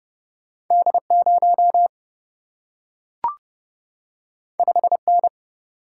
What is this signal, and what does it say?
Morse code, heard as a keyed tone, about 30 wpm, 700 Hz. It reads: D0 5N
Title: Single-Letter Single-Number 30wpm